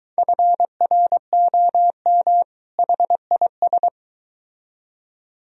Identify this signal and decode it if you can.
Morse code, heard as a keyed tone, about 23 wpm, 700 Hz. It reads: FROM HIS